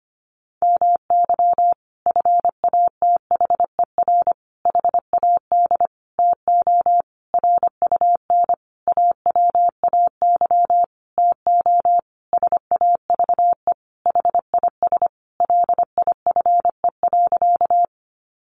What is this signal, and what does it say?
Morse code, heard as a keyed tone, about 25 wpm, 700 Hz. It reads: MY FATHER HAD TO RUN AWAY TO SAVE HIS LIFE.
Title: Sentences from Top 1100 Words 25wpm